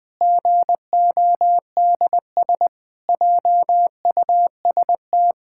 Morse code, heard as a keyed tone, about 20 wpm, 700 Hz. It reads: GODS JUST
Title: Pangram Challenge v2 - 20wpm